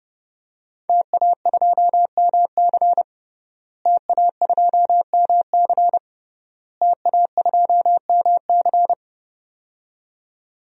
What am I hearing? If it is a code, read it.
Morse code, heard as a keyed tone, about 30 wpm, 700 Hz. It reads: TA2MC TA2MC TA2MC
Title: DXCC-100 Call Signs - 5 Characters - Repeated 3x - 30wpm